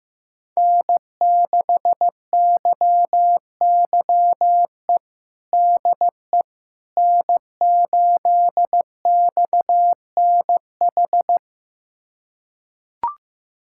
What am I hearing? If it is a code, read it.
Morse code, heard as a keyed tone, about 15 wpm, 700 Hz. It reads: N6YYE DE N8XNH
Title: USA Callsign DE Callsign - 5 Characters - 15wpm